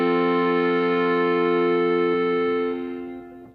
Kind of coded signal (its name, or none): telephone call progress tone